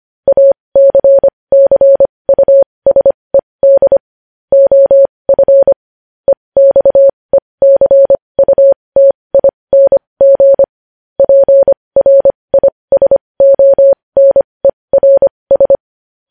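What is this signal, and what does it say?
Morse code, heard as a keyed tone, about 25 wpm, spaced as for 20 wpm, 550 Hz. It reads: ACCUSED OF EXECUTING PRISONERS
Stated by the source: News headlines sent in Morse code at 20.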